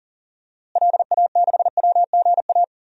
Morse code, heard as a keyed tone, about 40 wpm, 700 Hz. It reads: LA6WGA
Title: DXCC-100 Call Signs - 6 Characters - Repeated 2x - 40wpm